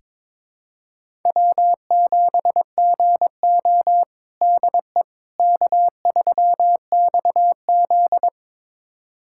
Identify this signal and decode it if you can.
Morse code, heard as a keyed tone, about 22 wpm, 700 Hz. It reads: W7GO DE K3XZ
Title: USA Callsign DE Callsign - 4 Characters - 22wpm